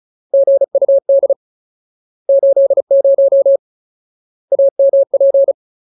Morse code, heard as a keyed tone, about 34 wpm, 550 Hz. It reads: GUD 80 AMP
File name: Ultimate Word List / 35WPM Ultimate